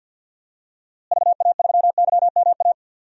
Morse code, heard as a keyed tone, about 52 wpm, 700 Hz. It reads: VA3YMA
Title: DXCC-100 Call Signs - 6 Characters - Repeated 2x - 50wpm